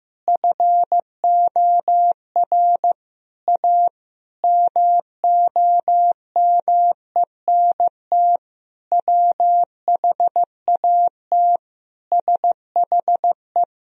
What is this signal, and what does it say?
Morse code, heard as a keyed tone, about 15 wpm, 700 Hz. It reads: FOR A MOMENT WHAT SHE
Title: Sentences from Top 1000 Words - Review Part 4 - 15wpm